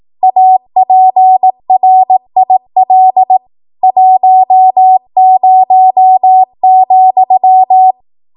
Morse code, heard as a keyed tone, about 18 wpm, 750 Hz. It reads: APRIL 10,
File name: Archive / 18 WPM / 18WPM